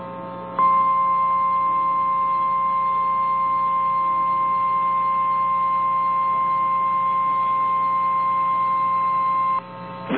Emergency Alert System (SAME) tones (the 1050 Hz attention tone). It is heard from 0.5 to 9.6 s.
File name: eas / KNCYKBIE / KNCYKBIE